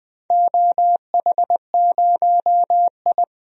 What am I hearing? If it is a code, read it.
Morse code, heard as a keyed tone, about 20 wpm, 700 Hz. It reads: OH0I
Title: DXCC-100 Call Signs - 4 Characters - 20wpm